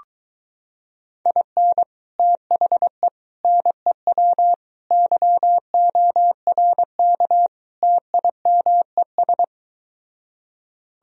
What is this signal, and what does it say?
Morse code, heard as a keyed tone, about 23 wpm, 700 Hz. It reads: IN THE NEW YORK TIMES